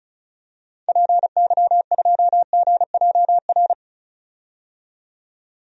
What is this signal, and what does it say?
Morse code, heard as a keyed tone, about 34 wpm, 700 Hz. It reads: PY2GJR
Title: DXCC-100 Call Signs - 6 Characters - 35wpm